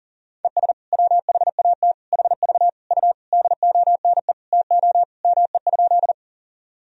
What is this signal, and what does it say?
Morse code, heard as a keyed tone, about 37 wpm, 700 Hz. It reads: ES WHAT HV U DONE TO ME?
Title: Sentences from Top 500 Words and Abbreviations 40wpm